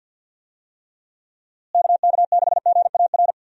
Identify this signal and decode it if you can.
Morse code, heard as a keyed tone, about 51 wpm, 700 Hz. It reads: KK6GAR
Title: USA Call Signs - 6 Characters - Repeated 2x - 50wpm